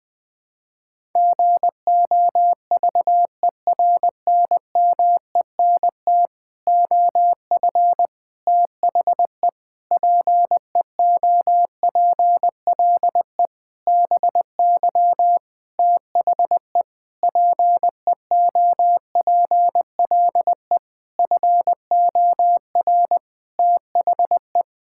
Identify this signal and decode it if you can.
Morse code, heard as a keyed tone, about 20 wpm, 700 Hz. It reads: GOVERNMENT OF THE PEOPLE BY THE PEOPLE FOR THE
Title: Sentences with 9-Word Phrases - Spoken Beginning - 20wpm